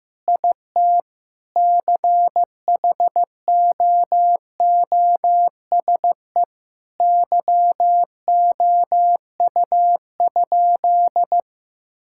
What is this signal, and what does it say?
Morse code, heard as a keyed tone, about 15 wpm, 700 Hz. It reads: IT CHOOSE YOU?
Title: Sentences from Top 1000 Words - Review Part 6 - 15wpm